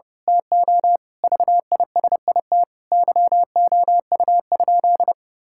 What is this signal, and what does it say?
Morse code, heard as a keyed tone, about 28 wpm, 700 Hz. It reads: TO VISIT YOU?
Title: Sentences from Top 1000 Words - Review Part 6 - 30wpm